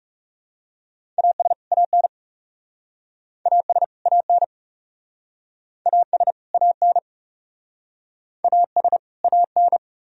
Morse code, heard as a keyed tone, about 40 wpm, 700 Hz. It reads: AS AN AS AN AS AN AS AN